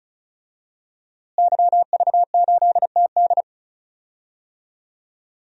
Morse code, heard as a keyed tone, about 36 wpm, 700 Hz. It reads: YV8TD
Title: DXCC-100 Call Signs - 5 Characters - Repeated 2x - 35wpm